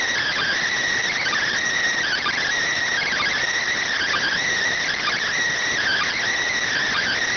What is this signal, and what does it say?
Emergency Alert System (SAME) tones (header bursts), heard as harsh buzzing bursts. It is heard from 0.2 to 6.2 s.